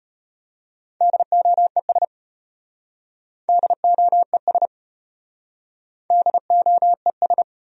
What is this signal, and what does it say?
Morse code, heard as a keyed tone, about 33 wpm, 700 Hz. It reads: DOES DOES DOES